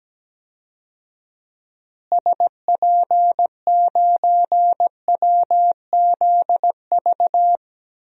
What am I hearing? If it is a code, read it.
Morse code, heard as a keyed tone, about 17 wpm, 700 Hz. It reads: SP9WZV
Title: DXCC-100 Call Signs - 6 Characters - Repeated 2x - 17wpm